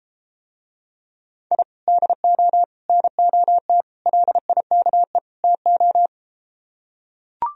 Morse code, heard as a keyed tone, about 33 wpm, 700 Hz. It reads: I DO NOT LIKE TO